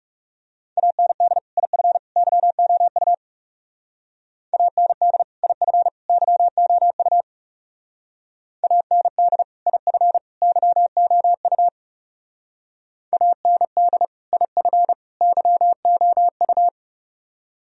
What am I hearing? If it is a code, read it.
Morse code, heard as a keyed tone, about 35 wpm, 700 Hz. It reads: AND IF YOU AND IF YOU AND IF YOU AND IF YOU
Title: Speed-Racing - Sets of 3 Words 30wpm